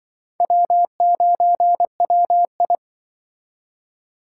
Morse code, heard as a keyed tone, about 24 wpm, 700 Hz. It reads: W9WI
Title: Speed Racing - USA Call Signs 20wpm